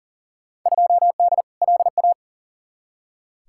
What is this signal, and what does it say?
Morse code, heard as a keyed tone, about 40 wpm, 700 Hz. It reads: 2D LA
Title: Field Day Exchanges - 40wpm